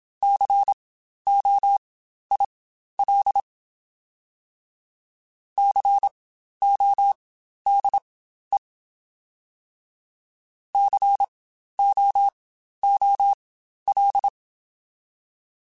Morse code, heard as a keyed tone, about 27 wpm, spaced as for 12 wpm, 800 Hz. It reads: COIL CODE COOL